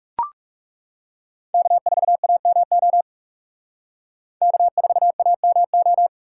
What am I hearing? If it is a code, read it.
Morse code, heard as a keyed tone, about 38 wpm, 700 Hz. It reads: K4AMO K4AMO